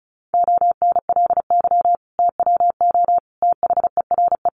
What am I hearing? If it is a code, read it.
Morse code, heard as a keyed tone, about 34 wpm, 700 Hz. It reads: ONLY TWO THERE
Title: Sentences from Top 200 Words 35wpm